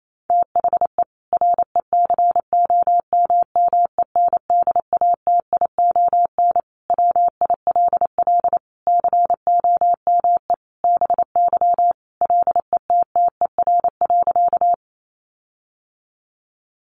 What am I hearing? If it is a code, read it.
Morse code, heard as a keyed tone, about 28 wpm, 700 Hz. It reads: THE RECOMMENDATION WILL COME BY LETTER.
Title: Sentences from Top 1400 Words 28wpm